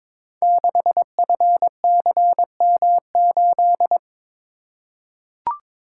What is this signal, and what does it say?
Morse code, heard as a keyed tone, about 22 wpm, 700 Hz. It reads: 6FCM8
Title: Code Groups - 5 Letter:Numbers - 22wpm